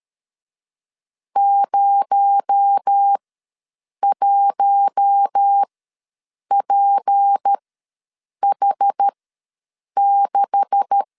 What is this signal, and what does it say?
Morse code, heard as a keyed tone, about 13 wpm, spaced as for 7 wpm, 800 Hz. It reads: 01PH6
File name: LEZIONI / 10 SIGNS / 10B INT ws 12wpm 3527